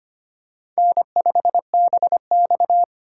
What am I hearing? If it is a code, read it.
Morse code, heard as a keyed tone, about 25 wpm, 700 Hz. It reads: N5BX
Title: USA Call Signs - 8x Character Spacing - 25wpm